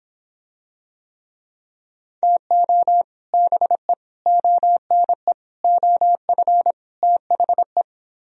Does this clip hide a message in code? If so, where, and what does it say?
Yes: Morse code from 2.2 to 7.8 s, heard as a keyed tone, about 26 wpm, 700 Hz. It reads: TO BE ONE OF THE